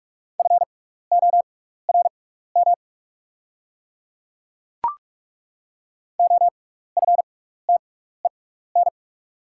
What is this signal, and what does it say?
Morse code, heard as a keyed tone, about 42 wpm, spaced as for 15 wpm, 700 Hz. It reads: FORM OFTEN
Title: 6x Character Spacing - Top 300 Words - 45wpm